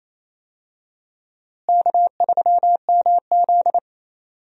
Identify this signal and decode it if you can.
Morse code, heard as a keyed tone, about 28 wpm, 700 Hz. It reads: K3MZ
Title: USA Call Signs - 4 Characters - Repeated 2x - 28wpm